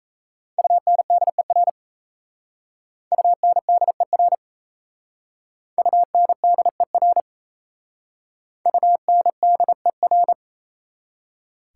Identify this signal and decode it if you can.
Morse code, heard as a keyed tone, about 34 wpm, 700 Hz. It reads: UNDER UNDER UNDER UNDER
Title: Speed-Racing Prefixes and Suffixes 28wpm